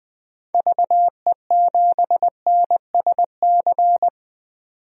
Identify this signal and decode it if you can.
Morse code, heard as a keyed tone, about 20 wpm, 700 Hz. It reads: VE7NSC
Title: DXCC-100 Call Signs - 6 Characters - 20wpm